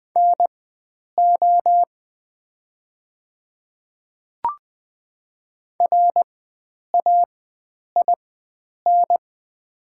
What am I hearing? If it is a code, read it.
Morse code, heard as a keyed tone, about 20 wpm, spaced as for 9 wpm, 700 Hz. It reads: NO RAIN
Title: 4x Character Spacing - Top 300 Words - 20wpm